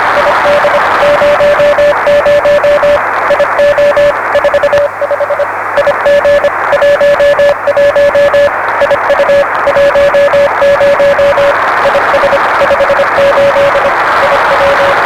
Morse code, heard as a keyed tone, about 25 wpm, 560 Hz, with radio noise. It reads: ID 00 IO 45 IG 11 IU 10 IS 58 IW
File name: RS7 Beacon 16 Feb 1985